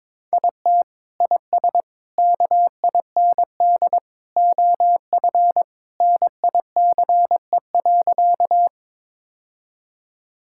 Morse code, heard as a keyed tone, about 22 wpm, 700 Hz. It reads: IT IS KIND OF NICE.
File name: practice / 22wpm / Sentences with 2-Word Phrases - 22wpm